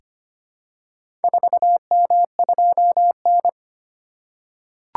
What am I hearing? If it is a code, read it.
Morse code, heard as a keyed tone, about 25 wpm, 700 Hz. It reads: 4M2N
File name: practice / 25wpm / Four Letter-Number Combinations - 25wpm